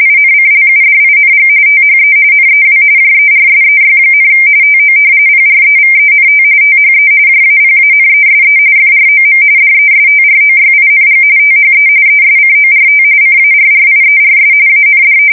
RTTY, about 45.45 baud. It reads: INCLUDING THE NEW JERSEY, TO PREVENT THE JAPANESE FROM DESTROYING THE ESCORT CARRIERS AND
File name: fsk170 / battleshipnj